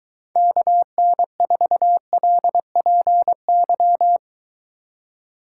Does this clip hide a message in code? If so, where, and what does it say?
Yes: Morse code from 0.4 to 4.2 s, heard as a keyed tone, about 22 wpm, 700 Hz. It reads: KN4LPY